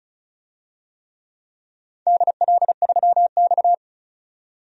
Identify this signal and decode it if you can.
Morse code, heard as a keyed tone, about 34 wpm, 700 Hz. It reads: DL3X
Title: DXCC-100 Call Signs - 4 Characters - 35wpm